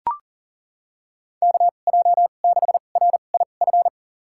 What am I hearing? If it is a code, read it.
Morse code, heard as a keyed tone, about 40 wpm, spaced as for 29 wpm, 700 Hz. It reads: KJ6RIF
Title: USA Call Signs - 6 Characters - 2x Character Spacing - 40wpm